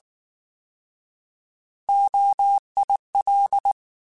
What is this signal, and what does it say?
Morse code, heard as a keyed tone, about 19 wpm, 800 Hz. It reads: OIL
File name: Alphabetical / Alphabetical mp3s / Os @18